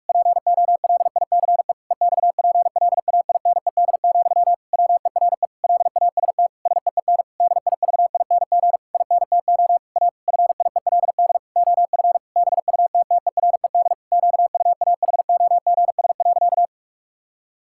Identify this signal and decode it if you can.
Morse code, heard as a keyed tone, about 41 wpm, 700 Hz. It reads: POLICE EXPLAINED, WERE LAST SEEN DIVING INTO A FIELD OF BUTTERED QUAHOGS.
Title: Pangram Challenge v2 - 45wpm